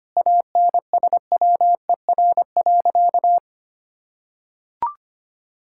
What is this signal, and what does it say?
Morse code, heard as a keyed tone, about 25 wpm, 700 Hz. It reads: ANSWER.
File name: practice / 25wpm / Sentences with 4-Word Phrases - 25wpm